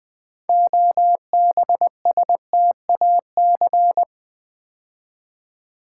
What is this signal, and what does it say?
Morse code, heard as a keyed tone, about 20 wpm, 700 Hz. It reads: OBSTAC
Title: Build A Word - 3 Letters At A Time - 8-Letter Words - 20wpm